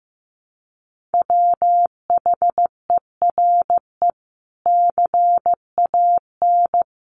Morse code, heard as a keyed tone, about 15 wpm, 700 Hz. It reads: WHERE CAN
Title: Sentences from Top 1500 Words 15wpm